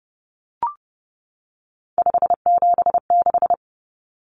Morse code, heard as a keyed tone, about 30 wpm, 700 Hz. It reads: 576
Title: Common Cut Numbers - 3-Digit 30wpm